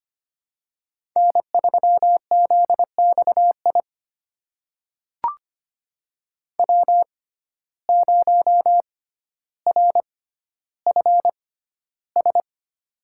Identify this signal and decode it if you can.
Morse code, heard as a keyed tone, about 25 wpm, spaced as for 13 wpm, 700 Hz. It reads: N3ZXI W0RFS
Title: USA Call Signs - 5 Characters - 6x Character Spacing - 25wpm